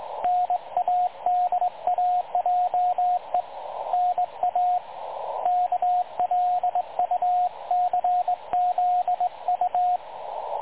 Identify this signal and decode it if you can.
Morse code, heard as a keyed tone, about 20 wpm, 720 Hz, with radio noise. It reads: NADAJE NA KLUCZU
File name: dlazaawansowanych / nagraniazpasma / websdr recording 3565.0kHz